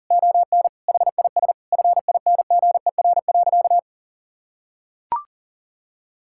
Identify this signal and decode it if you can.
Morse code, heard as a keyed tone, about 40 wpm, 700 Hz. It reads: ON HIS FINGER.
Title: Sentences from Top 1600 Words 40wpm